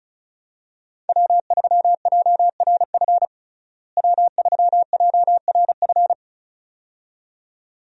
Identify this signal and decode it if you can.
Morse code, heard as a keyed tone, about 34 wpm, 700 Hz. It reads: W3JRF W3JRF
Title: USA Call Signs - 5 Characters - Repeated 2x - 35wpm